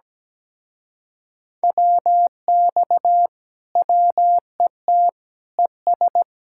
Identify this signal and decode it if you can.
Morse code, heard as a keyed tone, about 17 wpm, 700 Hz. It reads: WX WET ES
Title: Speed-Racing Common QSO Phrases - 17wpm